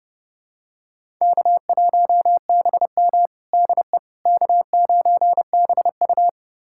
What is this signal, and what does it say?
Morse code, heard as a keyed tone, about 30 wpm, 700 Hz. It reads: K1BM DE K9BU
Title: USA Callsign DE Callsign - 4 Characters - 30wpm